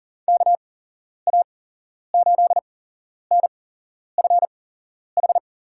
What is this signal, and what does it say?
Morse code, heard as a keyed tone, about 40 wpm, spaced as for 11 wpm, 700 Hz. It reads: KA8NFH
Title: USA Call Signs - 6 Characters - 8x Character Spacing - 40wpm